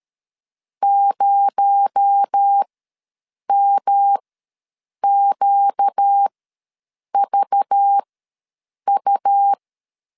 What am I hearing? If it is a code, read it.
Morse code, heard as a keyed tone, about 13 wpm, spaced as for 7 wpm, 800 Hz. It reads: 0MQVU
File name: LEZIONI / 7 GF9Z / 7 G ws 12wpm 3140